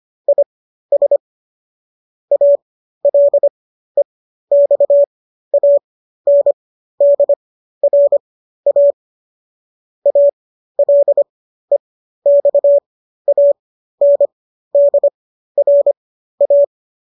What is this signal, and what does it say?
Morse code, heard as a keyed tone, about 25 wpm, spaced as for 13 wpm, 570 Hz. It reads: IS ALEXANDRA ALEXANDRA
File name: qso219 13